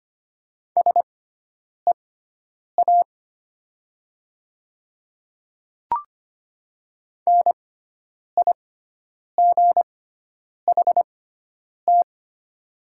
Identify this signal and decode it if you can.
Morse code, heard as a keyed tone, about 25 wpm, spaced as for 9 wpm, 700 Hz. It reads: SEA NIGHT
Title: 6x Character Spacing - Top 300 Words - 25wpm